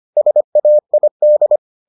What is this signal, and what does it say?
Morse code, heard as a keyed tone, about 25 wpm, 600 Hz. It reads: SAID